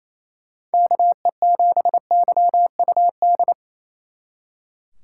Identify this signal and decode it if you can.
Morse code, heard as a keyed tone, about 32 wpm, 700 Hz. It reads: KE7YUD
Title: Speed Racing - USA 2x3 Call Signs 28wpm